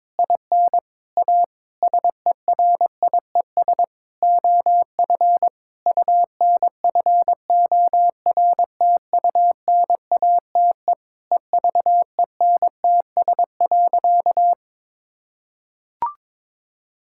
Morse code, heard as a keyed tone, about 22 wpm, 700 Hz. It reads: IN A SERIES OF UNFORTUNATE EVENTS.
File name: practice / 22wpm / Sentences with 8-Word Phrases - 22wpm